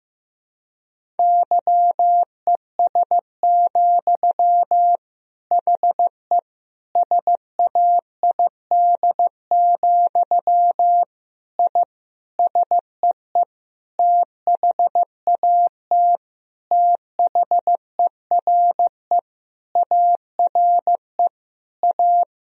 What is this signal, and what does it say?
Morse code, heard as a keyed tone, about 15 wpm, 700 Hz. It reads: YES, HE SAID, I SEE THAT THERE ARE A
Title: Sentences from Top 1000 Words - Review Part 1 - 15wpm